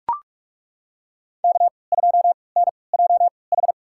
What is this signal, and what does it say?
Morse code, heard as a keyed tone, about 43 wpm, spaced as for 25 wpm, 700 Hz. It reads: K2NJH
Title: USA Call Signs - 5 Characters - 3x Character Spacing - 45wpm